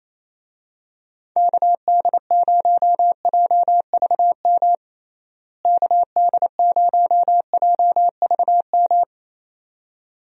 Morse code, heard as a keyed tone, about 28 wpm, 700 Hz. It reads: KD0JVM KD0JVM
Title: USA Call Signs - 6 Characters - Repeated 2x - 28wpm